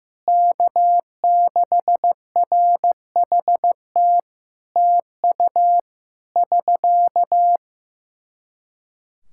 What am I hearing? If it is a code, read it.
Morse code, heard as a keyed tone, about 15 wpm, 700 Hz. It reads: K6RHT TU <SK>
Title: Practice QSO - Exchange 4 - Ending - 15wpm